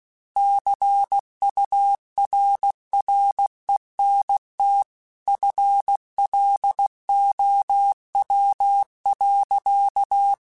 Morse code, heard as a keyed tone, about 16 wpm, 800 Hz. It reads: CURRENT FLOW.
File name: Technical Subjects / Technical Subjects-mp3s / XL @15